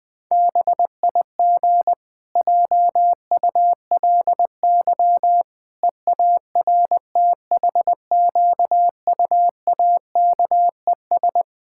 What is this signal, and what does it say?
Morse code, heard as a keyed tone, about 20 wpm, 700 Hz. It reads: BIG JULY EARTHQUAKES
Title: Pangram Challenge v2 - 20wpm